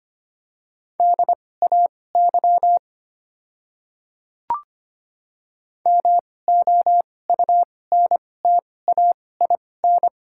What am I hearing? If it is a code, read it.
Morse code, heard as a keyed tone, about 25 wpm, spaced as for 18 wpm, 690 Hz. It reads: DAY MOUNTAIN
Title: 2x Character Spacing - Top 300 Words - 25wpm